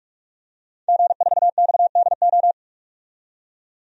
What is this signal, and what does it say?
Morse code, heard as a keyed tone, about 46 wpm, 700 Hz. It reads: G4XDO
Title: DXCC-100 Call Signs - 5 Characters - 45wpm